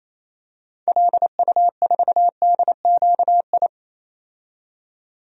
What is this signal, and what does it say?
Morse code, heard as a keyed tone, about 27 wpm, 700 Hz. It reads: LU4DQI
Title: DXCC-100 Call Signs - 6 Characters - 28wpm